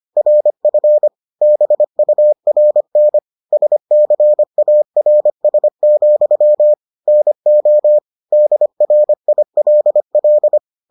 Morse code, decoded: RF BURN SCARS, NO DRILL

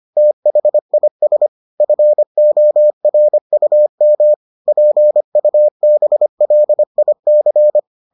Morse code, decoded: THIS FORUM PUBLIC

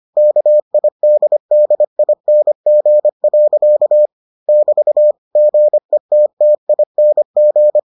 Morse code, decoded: KIDDING. <BT> GETTING

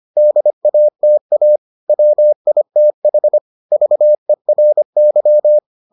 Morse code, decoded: DATA WITH VERY

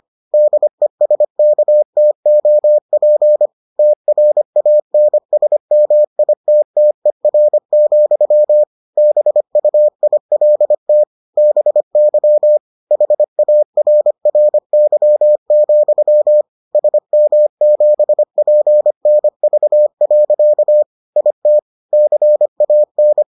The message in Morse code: DESKTOP TRANSMITTER, BUILT BY HARRY, SM7PNV. IT CAN

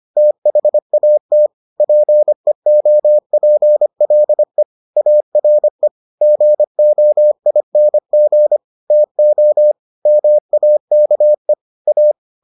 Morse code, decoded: THAT PEOPLE ARE GOING TO MAKE A

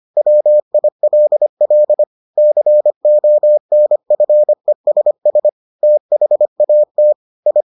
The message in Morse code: WILL CONFESS THAT I

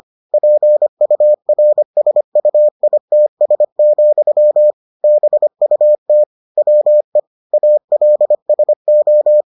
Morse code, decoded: PURSUITS, BUT WE ALSO

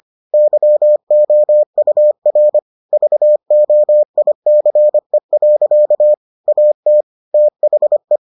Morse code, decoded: YOUR VOICE. AT THE

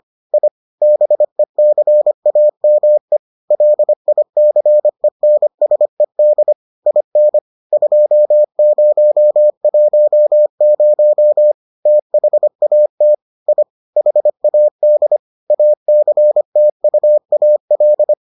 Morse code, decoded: I BECAME LICENSED IN 2010 THAT I HAD ACTUAL

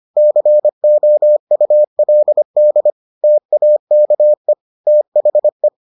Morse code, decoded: COULD TAKE THE

